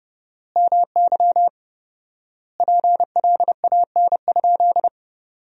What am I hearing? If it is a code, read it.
Morse code, heard as a keyed tone, about 30 wpm, 700 Hz. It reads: MY PLAN?
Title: 4x Word-Spacing - Sentences from Top 500 Words - 30wpm